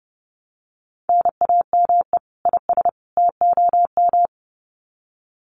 Morse code, heard as a keyed tone, about 30 wpm, 700 Hz. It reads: NAME IS TOM
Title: Common QSO Phrases 30wpm